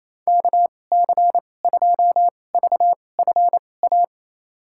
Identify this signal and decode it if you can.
Morse code, heard as a keyed tone, about 27 wpm, spaced as for 20 wpm, 700 Hz. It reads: KC2VFA
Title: USA Call Signs - 6 Characters - 2x Character Spacing - 28wpm